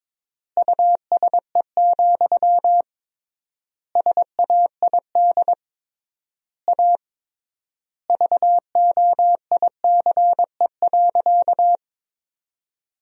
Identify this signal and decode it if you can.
Morse code, heard as a keyed tone, about 22 wpm, 700 Hz. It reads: USE, SAID A VOICE.
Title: 3x Word-Spacing - Sentences from Top 500 Words - 22wpm